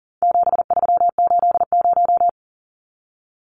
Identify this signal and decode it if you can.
Morse code, heard as a keyed tone, about 40 wpm, 700 Hz. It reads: 7380
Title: Numbers - 4-Digit 40wpm